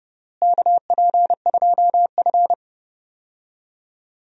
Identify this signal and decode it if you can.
Morse code, heard as a keyed tone, about 30 wpm, 700 Hz. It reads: KP2F
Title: DXCC-100 Call Signs - 4 Characters - Repeated 3x - 30wpm